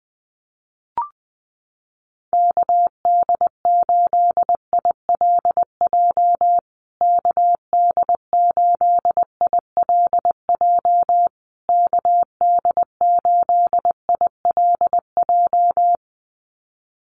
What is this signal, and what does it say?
Morse code, heard as a keyed tone, about 20 wpm, 700 Hz. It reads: KD8ILJ KD8ILJ KD8ILJ
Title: USA Call Signs Repeated 3x 20wpm